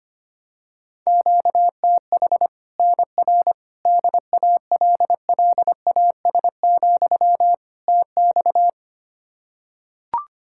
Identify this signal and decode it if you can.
Morse code, heard as a keyed tone, about 25 wpm, 700 Hz. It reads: QTH NR DALLAS, TX
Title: Practice QSO - Exchange 1 - Beginning - 25wpm